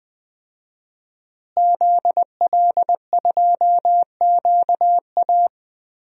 Morse code, heard as a keyed tone, about 20 wpm, 700 Hz. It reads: ZL2QA
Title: DXCC-100 Call Signs - 5 Characters - Repeated 2x - 20wpm